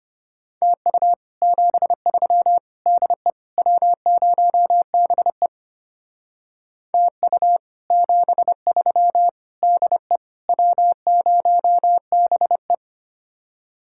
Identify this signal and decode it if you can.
Morse code, heard as a keyed tone, about 31 wpm, 700 Hz. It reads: TU 73 DE W0BE TU 73 DE W0BE
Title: Speed-Racing Common QSO Phrases Encore - 25wpm